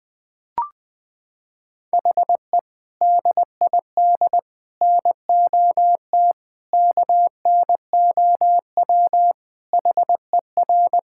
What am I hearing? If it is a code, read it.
Morse code, heard as a keyed tone, about 20 wpm, 700 Hz. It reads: HE DID NOT KNOW HER
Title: Sentences with 4-Word Phrases - 20wpm